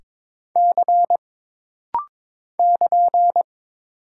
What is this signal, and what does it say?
Morse code, heard as a keyed tone, about 22 wpm, 700 Hz. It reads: C <KN>
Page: This rapid-fire timing is geared to push your ICR (Instant Character Recognition) skills.